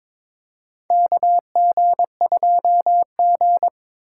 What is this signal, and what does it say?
Morse code, heard as a keyed tone, about 22 wpm, 700 Hz. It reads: KG2G
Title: USA Call Signs - 4 Characters - 3x Character Spacing - 22wpm